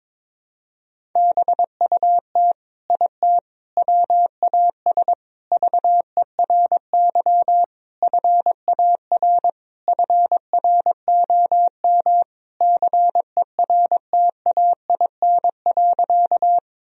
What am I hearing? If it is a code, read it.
Morse code, heard as a keyed tone, about 22 wpm, 700 Hz. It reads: BUT IT WAS VERY FAR FROM CERTAIN.
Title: Sentences from Top 1000 Words - Review Part 2 - 22wpm